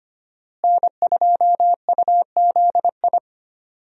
Morse code, heard as a keyed tone, about 25 wpm, 700 Hz. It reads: N2UZI
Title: USA Call Signs - 5 Characters - Repeated 2x - 25wpm